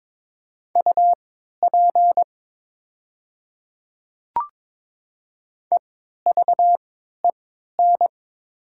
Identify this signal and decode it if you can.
Morse code, heard as a keyed tone, about 22 wpm, spaced as for 12 wpm, 700 Hz. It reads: UP EVEN
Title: 3x Character Spacing - Top 500 Words - 22wpm